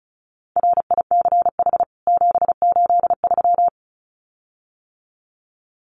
Morse code, heard as a keyed tone, about 34 wpm, 700 Hz. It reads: RICH 783
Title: CWT Exchange 35wpm